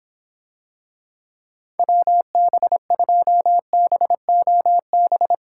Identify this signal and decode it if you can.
Morse code, heard as a keyed tone, about 25 wpm, 700 Hz. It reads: WB2BOB